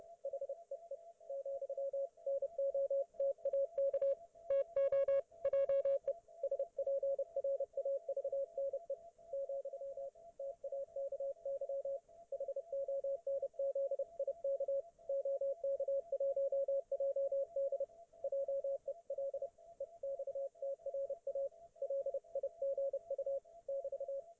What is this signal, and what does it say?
Morse code, heard as a keyed tone, about 29 wpm, 550 Hz, with radio noise. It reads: HEE, NOTAK TO JE SPRAVNE, TAKY HONZIK OK1JD JEL EXTRA LIGU <BT>